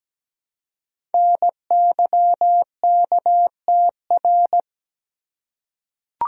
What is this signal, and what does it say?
Morse code, heard as a keyed tone, about 17 wpm, 700 Hz. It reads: NYKTR
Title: Code Groups - 5 Letters - 17wpm